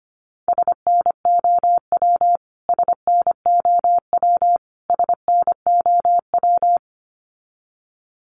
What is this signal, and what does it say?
Morse code, heard as a keyed tone, about 25 wpm, 700 Hz. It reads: SNOW SNOW SNOW
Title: Top 500 Words Repeated 25wpm